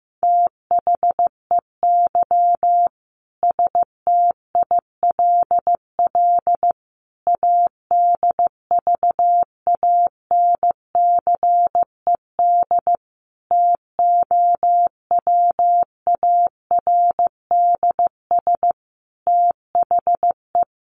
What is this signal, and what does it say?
Morse code, heard as a keyed tone, about 15 wpm, 700 Hz. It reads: THEY STILL ADVANCED TOWARDS THE
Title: Sentences from Top 1200 Words 15wpm